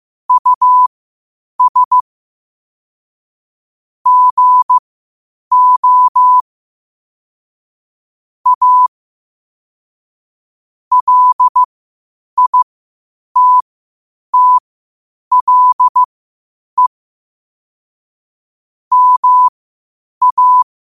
Morse code, heard as a keyed tone, about 15 wpm, spaced as for 8 wpm, 1000 Hz. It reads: US GO A LITTLE MA
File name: 8wpm